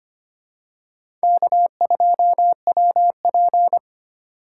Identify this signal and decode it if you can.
Morse code, heard as a keyed tone, about 25 wpm, 700 Hz. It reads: K2WP
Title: USA Call Signs - 4 Characters - 3x Character Spacing - 25wpm